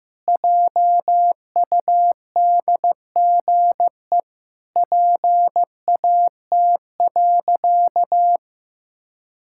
Morse code, heard as a keyed tone, about 15 wpm, 700 Hz. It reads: JUDGE PAT.
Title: Pangram Challenge v2 - 15wpm